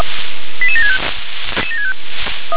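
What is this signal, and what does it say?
Five-tone selective call (ZVEI), heard as quick runs of five tones, tones of about 70 ms. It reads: 88765
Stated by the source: ZVEI five-tone call 88832 received on SDR.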